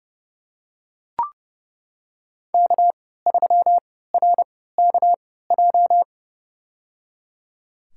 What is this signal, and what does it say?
Morse code, heard as a keyed tone, about 30 wpm, spaced as for 17 wpm, 700 Hz. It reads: K3RKJ
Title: USA Call Signs - 5 Characters - 3x Character Spacing - 30wpm